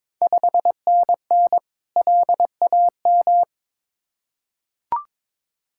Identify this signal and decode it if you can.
Morse code, heard as a keyed tone, about 22 wpm, 700 Hz. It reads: 5NN LAM
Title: Ontario QSO Party Exchange 22wpm